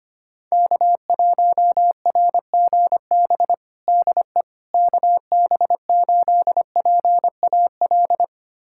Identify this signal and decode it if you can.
Morse code, heard as a keyed tone, about 25 wpm, 700 Hz. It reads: K1RGB DE KB8PAL
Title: USA Callsign DE Callsign 25wpm